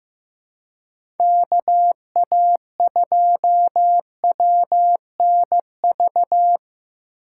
Morse code, heard as a keyed tone, about 15 wpm, 700 Hz. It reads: KA2WNV
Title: USA Call Signs - 2x Character Spacing - 15wpm